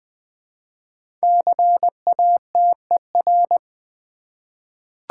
Morse code, heard as a keyed tone, about 20 wpm, 700 Hz. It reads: CATER
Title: 5 Letter Words - First 3 Letters Given - 20wpm